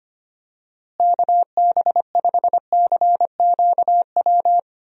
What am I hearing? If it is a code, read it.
Morse code, heard as a keyed tone, about 25 wpm, 700 Hz. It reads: KB5CQW
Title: USA Call Signs - 6 Characters - Repeated 2x - 25wpm